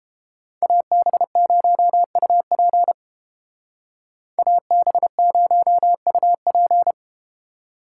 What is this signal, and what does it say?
Morse code, heard as a keyed tone, about 32 wpm, 700 Hz. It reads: AB0UP AB0UP